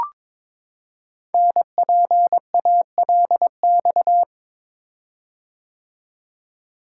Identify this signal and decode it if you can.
Morse code, heard as a keyed tone, about 22 wpm, 700 Hz. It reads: NPALX
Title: Code Groups - 5 Letter:Numbers - 22wpm